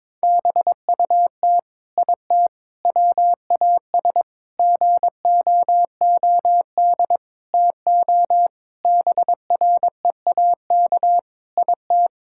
Morse code, heard as a keyed tone, about 22 wpm, 700 Hz. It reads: BUT IT WAS GOOD TO BREAK IT
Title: Sentences from Top 1000 Words - Review Part 6 - 22wpm